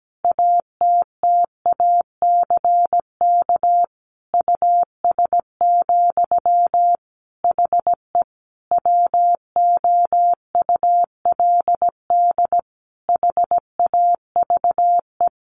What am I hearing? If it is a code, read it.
Morse code, heard as a keyed tone, about 17 wpm, 700 Hz. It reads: ATTACK US, HE WOULD HAVE
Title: Sentences from Top 1500 Words 17wpm